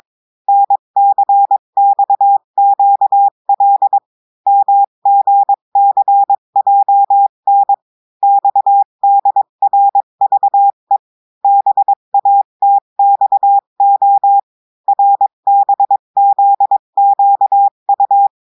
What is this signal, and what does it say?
Morse code, heard as a keyed tone, about 22 wpm, 800 Hz. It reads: NCXQL MGCJN XDRVE BATXO RBZQU
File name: cours-du-mercredi / audios / cours-853-du-2-mai-2024-etxte chiffre et texte clair-a-20-wpm